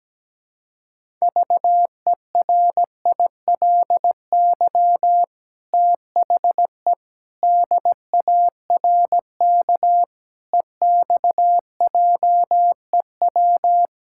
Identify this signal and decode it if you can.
Morse code, heard as a keyed tone, about 17 wpm, 700 Hz. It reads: VERILY THE DARK EXJEW
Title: Pangram Challenge v2 - 17wpm